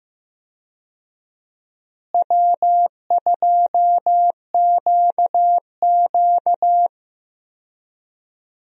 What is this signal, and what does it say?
Morse code, heard as a keyed tone, about 15 wpm, 700 Hz. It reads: W2QQ
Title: USA Call Signs - 4 Characters - Repeated 3x - 15wpm